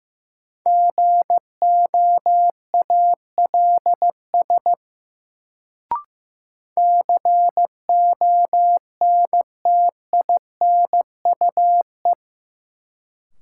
Morse code, heard as a keyed tone, about 15 wpm, 700 Hz. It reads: GOALS CONTINUE
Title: Sentences from Top 1000 Words - Review Part 6 - New Word Drill - 15wpm